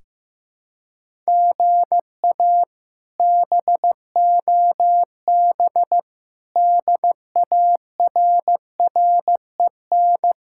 Morse code, heard as a keyed tone, about 15 wpm, 700 Hz. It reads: GA BOB DARREN
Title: K1USN SST QSO Practice - 15wpm